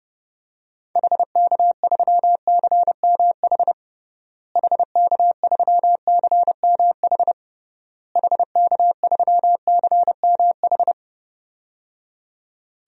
Morse code, heard as a keyed tone, about 30 wpm, 700 Hz. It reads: HK3CMH HK3CMH HK3CMH
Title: DXCC-100 Call Signs - 6 Characters - Repeated 3x - 30wpm